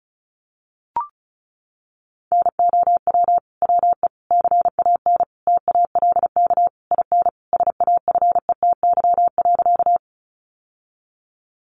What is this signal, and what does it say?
Morse code, heard as a keyed tone, about 34 wpm, 700 Hz. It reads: NOW WE CAN TALK IN SAFETY.
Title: Sentences from Top 700 Words 35wpm